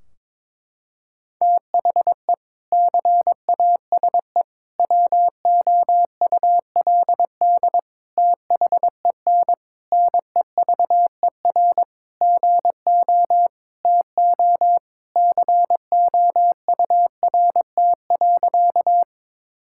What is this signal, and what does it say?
Morse code, heard as a keyed tone, about 22 wpm, 700 Hz. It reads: THE CASE WOULD THEN NEVER GO TO COURT.
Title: Sentences from Top 1600 Words 22wpm